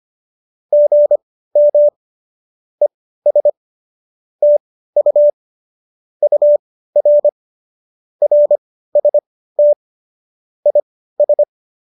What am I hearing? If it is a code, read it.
Morse code, heard as a keyed tone, about 25 wpm, spaced as for 15 wpm, 590 Hz. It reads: GM ES TU UR RST IS
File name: qso204 15